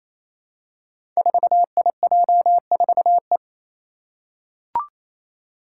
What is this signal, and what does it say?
Morse code, heard as a keyed tone, about 28 wpm, 700 Hz. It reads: 4IJ4E
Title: Five Challenging Letter-Number Combinations 28wpm